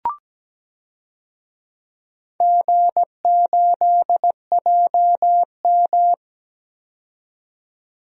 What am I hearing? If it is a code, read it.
Morse code, heard as a keyed tone, about 17 wpm, 700 Hz. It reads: G8JM
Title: DXCC-100 Call Signs - 4 Characters - 17wpm